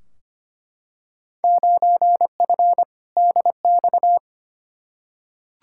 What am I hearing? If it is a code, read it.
Morse code, heard as a keyed tone, about 25 wpm, 700 Hz. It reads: 9F DX
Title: Field Day Exchanges Encore - 25wpm